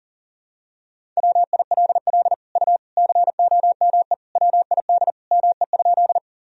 Morse code, heard as a keyed tone, about 37 wpm, 700 Hz. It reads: WILL U COME WID ME?
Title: Sentences from Top 200 Words and Abbreviations 40wpm